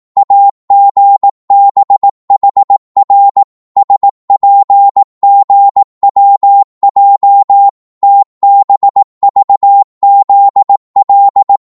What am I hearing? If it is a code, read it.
Morse code, heard as a keyed tone, about 18 wpm, 800 Hz. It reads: AGBHR SPGWJ TBVZL
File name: Morse / ltr-18wpm